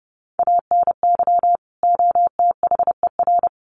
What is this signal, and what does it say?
Morse code, heard as a keyed tone, about 30 wpm, 700 Hz. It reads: ANY OTHER
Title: Sentences from Top 1100 Words 30wpm